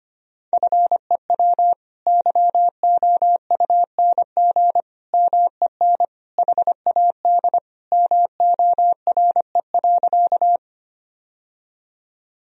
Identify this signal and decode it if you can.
Morse code, heard as a keyed tone, about 25 wpm, 700 Hz. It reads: FEW YOUNG MEN HAD MORE.
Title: Sentences from Top 1000 Words - Review Part 1 - 25wpm